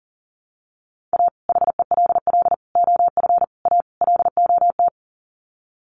Morse code, heard as a keyed tone, about 40 wpm, 700 Hz. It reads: A HELL OF A LOT
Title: Sets of 5 Words Encore 40wpm